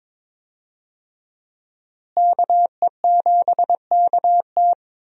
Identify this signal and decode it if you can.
Morse code, heard as a keyed tone, about 21 wpm, 700 Hz. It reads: KE7KT
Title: USA Call Signs - 5 Characters - Repeated 3x - 22wpm